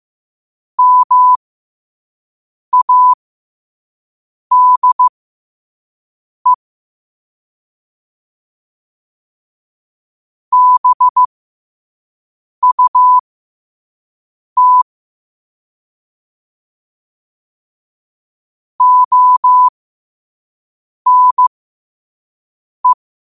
Morse code, heard as a keyed tone, about 15 wpm, spaced as for 5 wpm, 1000 Hz. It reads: MADE BUT ONE
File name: qotdfiles / 5wpm